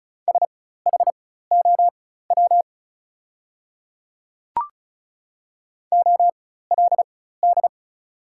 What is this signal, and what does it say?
Morse code, heard as a keyed tone, about 36 wpm, spaced as for 16 wpm, 700 Hz. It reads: SHOW OLD
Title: 4x Character Spacing - Top 200 Words - 35wpm